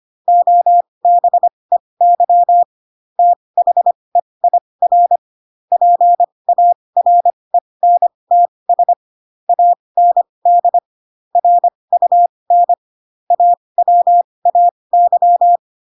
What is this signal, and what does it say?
Morse code, heard as a keyed tone, about 25 wpm, spaced as for 20 wpm, 700 Hz. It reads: OBEY THEIR PARENTS AND RUN AWAY
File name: Pinocchio / Pinocchio ch4 15 25 ik1hge